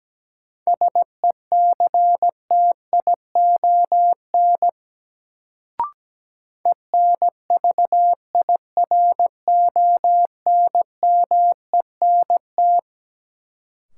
Morse code, heard as a keyed tone, about 17 wpm, 700 Hz. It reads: SECTION ENVIRONMENT
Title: Sentences from Top 1000 Words - Review Part 5 - New Word Drill - 17wpm